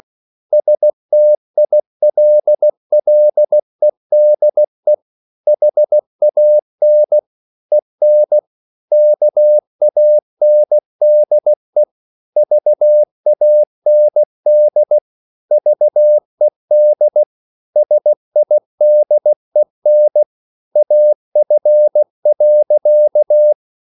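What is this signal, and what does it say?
Morse code, heard as a keyed tone, about 16 wpm, 600 Hz. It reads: STILLEDE HAN EN KANDE VAND VED SIDEN AF.
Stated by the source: Slow (QRS) Morse code at 16 wpm, from CW med Gnister.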